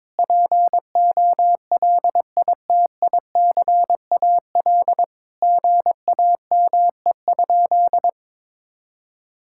Morse code, heard as a keyed tone, about 22 wpm, 700 Hz. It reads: POLITICAL GAME?
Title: Sentences from Top 1000 Words - Review Part 5 - 22wpm